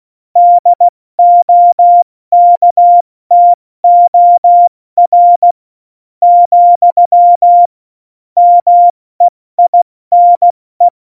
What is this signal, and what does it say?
Morse code, heard as a keyed tone, about 16 wpm, spaced as for 13 wpm, 700 Hz. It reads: DOKTOR, MEINE
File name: Morsekurs / Witze / Witz 16. Januar 2018, 70 F80